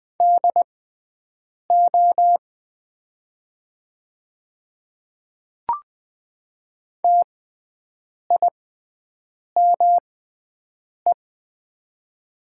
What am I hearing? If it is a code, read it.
Morse code, heard as a keyed tone, about 20 wpm, spaced as for 7 wpm, 700 Hz. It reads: DO TIME
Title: 6x Character Spacing - Top 100 Words - 20wpm